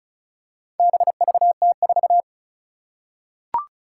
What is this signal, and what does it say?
Morse code, heard as a keyed tone, about 36 wpm, 700 Hz. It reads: BVT4
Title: Four Challenging Letter-Number Combinations 35wpm